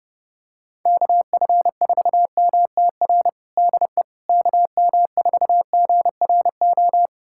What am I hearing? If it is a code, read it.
Morse code, heard as a keyed tone, about 30 wpm, 700 Hz. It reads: KF4MTR DE KM4GRO
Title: USA Callsign DE Callsign 30wpm